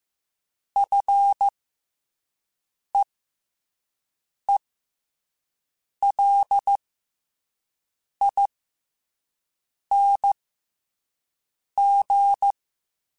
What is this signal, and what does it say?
Morse code, heard as a keyed tone, about 15 wpm, spaced as for 5 wpm, 800 Hz. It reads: FEELING
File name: F and L / F and L mp3 / F and L Prefixes 3x2 Twice @5